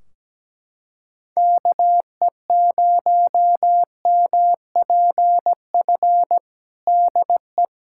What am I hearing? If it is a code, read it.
Morse code, heard as a keyed tone, about 17 wpm, 700 Hz. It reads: KE0MPF DE
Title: Practice QSO - Exchange 4 - Ending - 17wpm